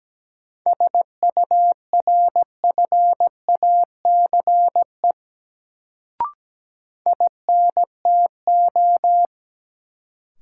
Morse code, heard as a keyed tone, about 17 wpm, 700 Hz. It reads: SURFACE INTO
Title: Sentences from Top 600 Words - New Word Drill - 17wpm